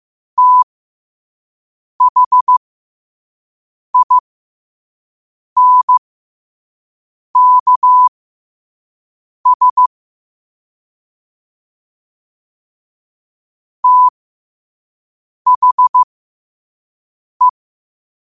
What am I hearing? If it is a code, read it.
Morse code, heard as a keyed tone, about 15 wpm, spaced as for 5 wpm, 1000 Hz. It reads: THINKS THE
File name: qotdfiles / 5wpm